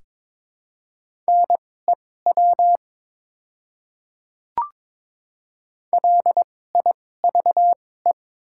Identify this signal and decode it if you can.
Morse code, heard as a keyed tone, about 22 wpm, spaced as for 16 wpm, 700 Hz. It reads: NEW LIVE
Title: 2x Character Spacing - Top 200 Words - 22wpm